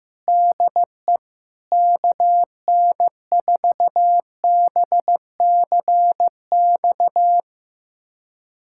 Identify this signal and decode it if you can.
Morse code, heard as a keyed tone, about 15 wpm, 700 Hz. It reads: DE KN4BCX
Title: USA Callsign DE Callsign 15wpm